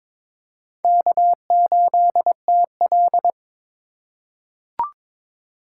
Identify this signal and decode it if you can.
Morse code, heard as a keyed tone, about 22 wpm, 700 Hz. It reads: K8TL
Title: USA Call Signs - 4 Characters - 2x Character Spacing - 22wpm